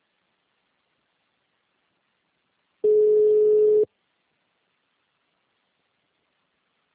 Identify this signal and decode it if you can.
Ringback tone (Europe, 425 Hz).
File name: out- -2021- - - .27846